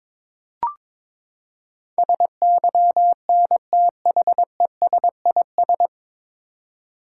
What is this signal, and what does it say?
Morse code, heard as a keyed tone, about 22 wpm, 700 Hz. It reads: SYNTHESIS
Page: Morse Code Ninja practice 22wpm